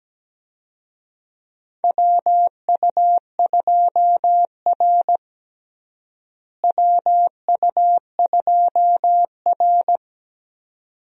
Morse code, heard as a keyed tone, about 17 wpm, 700 Hz. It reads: WU2R WU2R
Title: USA Call Signs - 4 Characters - Repeated 2x - 17wpm